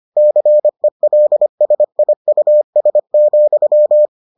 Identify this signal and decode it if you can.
Morse code, heard as a keyed tone, about 25 wpm, 600 Hz. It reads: CELSIUS,